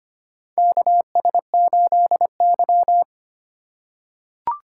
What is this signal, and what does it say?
Morse code, heard as a keyed tone, about 25 wpm, 700 Hz. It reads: KS8Y
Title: USA Call Signs - 4 Characters - 6x Character Spacing - 25wpm